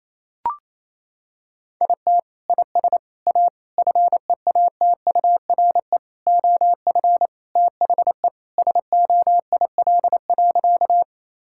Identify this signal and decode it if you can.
Morse code, heard as a keyed tone, about 28 wpm, 700 Hz. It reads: IT IS A FEATURE OF THE SOIL.
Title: Sentences from Top 1000 Words - Review Part 5 - 28wpm